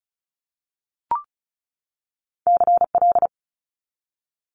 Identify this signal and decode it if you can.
Morse code, heard as a keyed tone, about 36 wpm, 700 Hz. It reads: CL
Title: Multi-letter Phonemes 35wpm